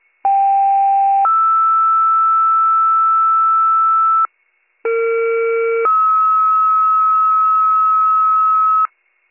Two-tone page, heard as a one-second tone, then a three-second tone, about 790 then 1320 Hz, about 460 then 1290 Hz. It is heard from 0.2 to 8.9 s.